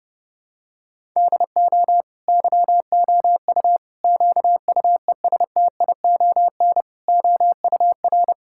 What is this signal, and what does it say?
Morse code, heard as a keyed tone, about 32 wpm, 700 Hz. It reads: DO YOU QUESTION OUR
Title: Sentences from Top 1000 Words - Review Part 5 - 30wpm